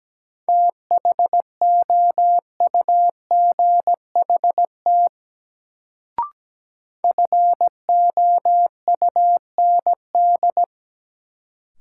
Morse code, heard as a keyed tone, about 17 wpm, 700 Hz. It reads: THOUGHT FOUND
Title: Sentences from Top 300 Words - New Word Drill - 17wpm